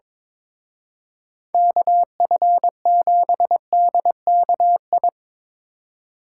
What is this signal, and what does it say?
Morse code, heard as a keyed tone, about 25 wpm, 700 Hz. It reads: KF7DKI
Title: Speed Racing - USA 2x3 Call Signs 22wpm